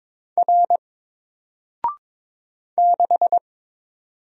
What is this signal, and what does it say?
Morse code, heard as a keyed tone, about 22 wpm, 700 Hz. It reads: R 6
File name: Single-Letter-Number-22wpm